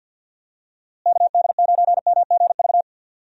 Morse code, heard as a keyed tone, about 51 wpm, 700 Hz. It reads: KD9MGV
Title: USA Call Signs - 6 Characters - Repeated 2x - 50wpm